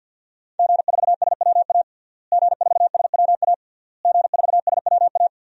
Morse code, heard as a keyed tone, about 51 wpm, 700 Hz. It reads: G4SWA G4SWA G4SWA
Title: DXCC-100 Call Signs - 5 Characters - Repeated 3x - 50wpm